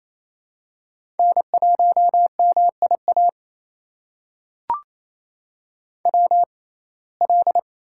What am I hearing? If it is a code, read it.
Morse code, heard as a keyed tone, about 28 wpm, spaced as for 20 wpm, 700 Hz. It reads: N1MIA WL
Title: USA Call Signs - 5 Characters - 6x Character Spacing - 28wpm